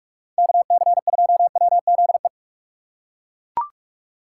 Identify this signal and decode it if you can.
Morse code, heard as a keyed tone, about 41 wpm, 700 Hz. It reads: KC2WZE